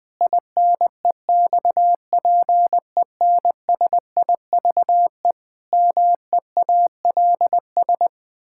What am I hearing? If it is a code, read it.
Morse code, heard as a keyed tone, about 19 wpm, 700 Hz. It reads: INEXPENSIVE MEALS
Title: Pangram Challenge v2 - 20wpm